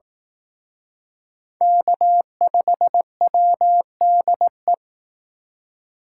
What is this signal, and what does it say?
Morse code, heard as a keyed tone, about 18 wpm, 700 Hz. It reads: K5WDE
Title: Speed Racing - USA 1x3 Call Signs 15wpm